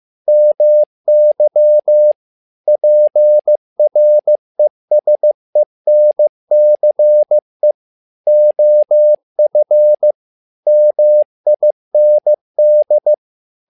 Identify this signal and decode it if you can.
Morse code, heard as a keyed tone, about 15 wpm, 600 Hz. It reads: MY PRESENCE OF MIND
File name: learning center / tower trouble / tower trouble 15wpm